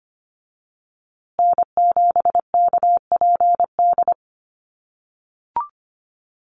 Morse code, heard as a keyed tone, about 25 wpm, 700 Hz. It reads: N7KPD
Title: USA Call Signs Repeated 3x 25wpm